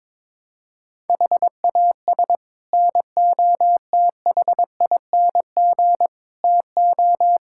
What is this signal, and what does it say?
Morse code, heard as a keyed tone, about 22 wpm, 700 Hz. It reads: HAS NOTHING TO
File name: practice / 22wpm / Sets of 6 Words - 22wpm